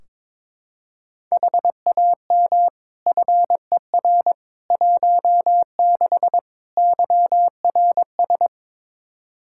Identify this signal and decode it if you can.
Morse code, heard as a keyed tone, about 22 wpm, 700 Hz. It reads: HAM FER 16 YRS
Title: Practice QSO - Exchange 3 - More Info - 22wpm